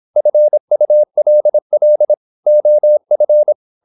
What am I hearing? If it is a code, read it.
Morse code, heard as a keyed tone, about 26 wpm, 600 Hz. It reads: FULL OF
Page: Never 26wpm | CW med Gnister